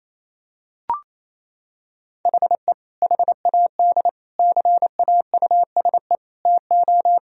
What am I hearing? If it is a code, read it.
Morse code, heard as a keyed tone, about 28 wpm, 700 Hz. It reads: HE HAD CAUSE TO
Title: Sentences from Top 1000 Words - Review Part 1 - 28wpm